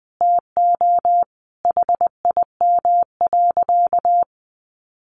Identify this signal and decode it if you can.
Morse code, heard as a keyed tone, about 20 wpm, 700 Hz. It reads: TO HIM.
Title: Sentences from Top 1200 Words 20wpm